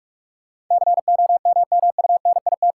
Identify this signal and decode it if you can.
Morse code, heard as a keyed tone, about 47 wpm, 700 Hz. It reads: COMMUNIT
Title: Sentences from Top 1200 Words - New Word Drill Alt - 45wpm